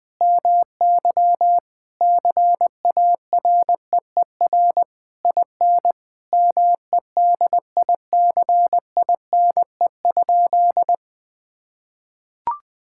Morse code, heard as a keyed tone, about 20 wpm, 700 Hz. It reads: MY CAREER IN MEDICINE?
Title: Sentences from Top 1000 Words - Review Part 6 - 20wpm